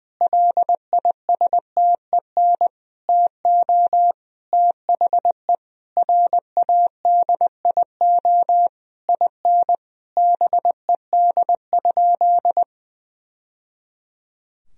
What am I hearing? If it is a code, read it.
Morse code, heard as a keyed tone, about 21 wpm, 700 Hz. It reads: LISTEN TO THE RADIO IN BED?
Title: Sentences from Top 1000 Words - Review Part 3 - 20wpm